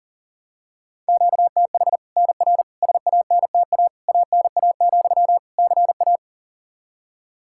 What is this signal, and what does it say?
Morse code, heard as a keyed tone, about 40 wpm, 700 Hz. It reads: QTH NR SANTA ANA, CA
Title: Practice QSO - Exchange 1 - Beginning - 40wpm